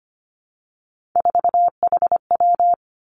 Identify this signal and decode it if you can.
Morse code, heard as a keyed tone, about 25 wpm, 700 Hz. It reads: 4HW
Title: Three Challenging Letter-Number Combinations 25wpm